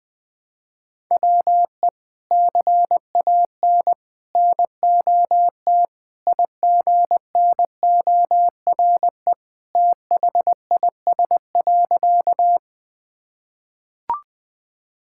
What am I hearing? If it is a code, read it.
Morse code, heard as a keyed tone, about 20 wpm, 700 Hz. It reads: WE CAN NOT IGNORE THIS.
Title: Sentences with 3-Word Phrases - Spoken Beginning - 20wpm